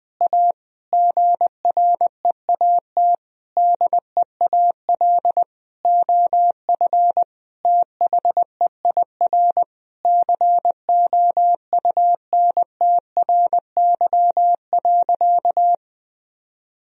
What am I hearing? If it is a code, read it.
Morse code, heard as a keyed tone, about 20 wpm, 700 Hz. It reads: A GREAT DEAL OF THEIR COUNTRY.
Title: Sentences from Top 1000 Words - Review Part 1 - 20wpm